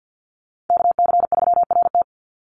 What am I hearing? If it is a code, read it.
Morse code, heard as a keyed tone, about 48 wpm, 700 Hz. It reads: KC3RT
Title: NA SOTA Chaser Callsigns 50wpm